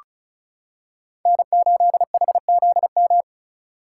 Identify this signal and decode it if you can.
Morse code, heard as a keyed tone, about 34 wpm, 700 Hz. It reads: N8HZM
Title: K1USN SST QSO Practice - 35wpm